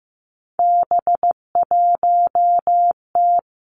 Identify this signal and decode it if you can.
Morse code, heard as a keyed tone, about 15 wpm, 700 Hz. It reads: B1T
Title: Three Challenging Letter-Number Combinations 15wpm